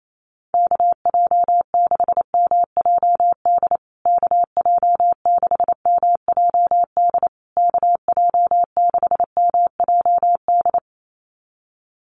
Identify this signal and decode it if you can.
Morse code, heard as a keyed tone, about 28 wpm, 690 Hz. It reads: KJ6MJD KJ6MJD KJ6MJD
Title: USA Call Signs Repeated 3x 28wpm